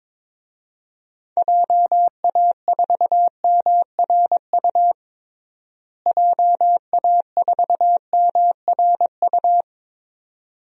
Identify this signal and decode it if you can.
Morse code, heard as a keyed tone, about 22 wpm, 700 Hz. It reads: JA4MRU JA4MRU
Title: DXCC-100 Call Signs - 6 Characters - Repeated 2x - 22wpm